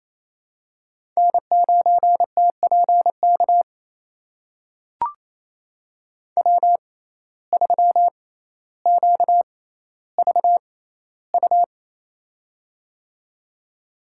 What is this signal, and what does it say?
Morse code, heard as a keyed tone, about 27 wpm, spaced as for 14 wpm, 700 Hz. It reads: N9TPK W3QVU
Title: USA Call Signs - 5 Characters - 6x Character Spacing - 28wpm